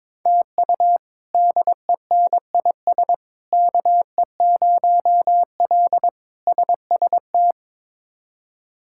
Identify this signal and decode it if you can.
Morse code, heard as a keyed tone, about 22 wpm, 700 Hz. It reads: TU DENIS KE0L SST
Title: K1USN SST QSO Practice - 22wpm